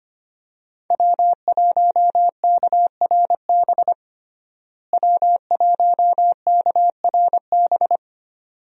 Morse code, heard as a keyed tone, about 25 wpm, 700 Hz. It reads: W1KRB W1KRB
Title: USA Call Signs - 5 Characters - Repeated 2x - 25wpm